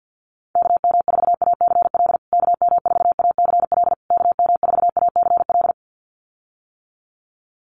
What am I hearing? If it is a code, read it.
Morse code, heard as a keyed tone, about 50 wpm, 700 Hz. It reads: KM4ACL KM4ACL KM4ACL
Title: USA Call Signs Repeated 3x 50wpm